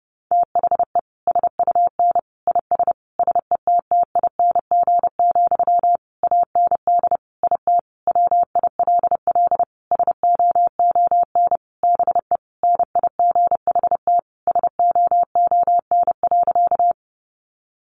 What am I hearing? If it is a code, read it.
Morse code, heard as a keyed tone, about 30 wpm, 700 Hz. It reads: THE SUN IS SETTING, AND IT WILL SOON BE NIGHT SOON.
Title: Sentences from Top 1300 Words 30wpm